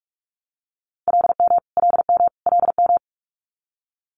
Morse code, heard as a keyed tone, about 46 wpm, 700 Hz. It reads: LM LM LM
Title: Multi-letter Phonemes Repeated 3x 45wpm